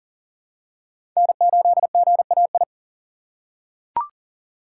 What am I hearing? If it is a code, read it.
Morse code, heard as a keyed tone, about 40 wpm, 690 Hz. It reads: N8GAI
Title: USA Call Signs - 2x Character Spacing - 40wpm